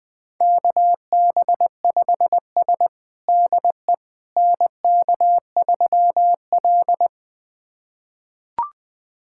Morse code, heard as a keyed tone, about 20 wpm, 700 Hz. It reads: KB5S DE NK3L
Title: USA Callsign DE Callsign - 4 Characters - 20wpm